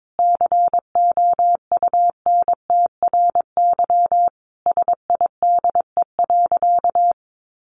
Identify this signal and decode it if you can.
Morse code, heard as a keyed tone, about 22 wpm, 700 Hz. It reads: COUNTRYSIDE.
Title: Sentences from Top 1500 Words 22wpm